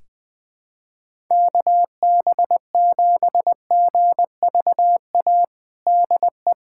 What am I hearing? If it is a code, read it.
Morse code, heard as a keyed tone, about 20 wpm, 700 Hz. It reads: KB7GVA DE
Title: USA Callsign DE Callsign - 6 Characters - 20wpm